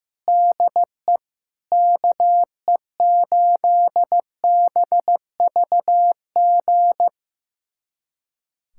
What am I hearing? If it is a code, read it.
Morse code, heard as a keyed tone, about 15 wpm, 700 Hz. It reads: DE KE8BVG
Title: USA Callsign DE Callsign - 6 Characters - 15wpm